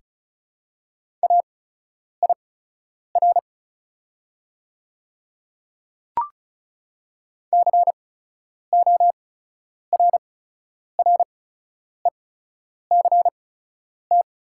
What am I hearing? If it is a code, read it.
Morse code, heard as a keyed tone, about 34 wpm, spaced as for 10 wpm, 690 Hz. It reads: AIR CORRECT
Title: 8x Character Spacing - Top 500 Words - 35wpm